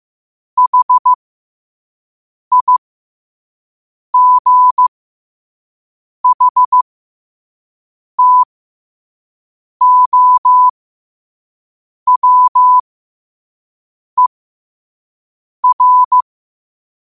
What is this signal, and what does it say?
Morse code, heard as a keyed tone, about 15 wpm, spaced as for 5 wpm, 1000 Hz. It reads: HIGHTOWER
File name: qotdfiles / 5wpm